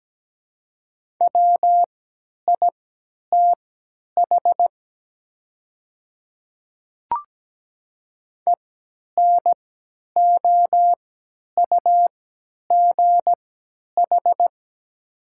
Morse code, heard as a keyed tone, about 17 wpm, spaced as for 10 wpm, 700 Hz. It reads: WITH ENOUGH
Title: 3x Character Spacing - Top 500 Words - 17wpm